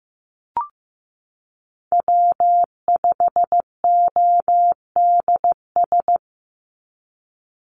Morse code, heard as a keyed tone, about 15 wpm, 700 Hz. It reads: W5ODS
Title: NA SOTA Chaser Callsigns 15wpm